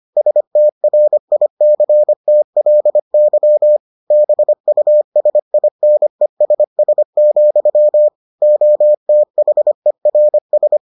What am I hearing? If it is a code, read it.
Morse code, heard as a keyed tone, about 25 wpm, 600 Hz. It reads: STRICTLY BUSINESS, OTHERS